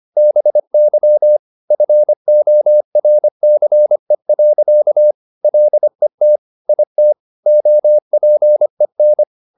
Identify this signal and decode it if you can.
Morse code, heard as a keyed tone, about 25 wpm, 600 Hz. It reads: BY FORCE. LET IT OPEN
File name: the old man and the sea part 1 25wpm